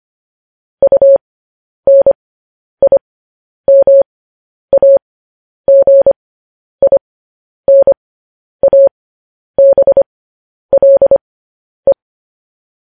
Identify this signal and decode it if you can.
Morse code, heard as a keyed tone, about 25 wpm, spaced as for 10 wpm, 550 Hz. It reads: UNIMAGINABLE
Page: news-headlines-in-morse-code-at-10-wpm